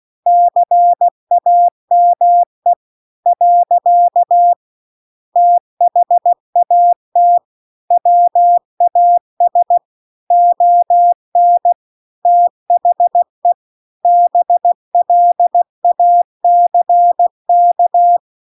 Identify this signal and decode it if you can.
Morse code, heard as a keyed tone, about 16 wpm, 700 Hz. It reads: CAME . THAT WAS ON THE BLACK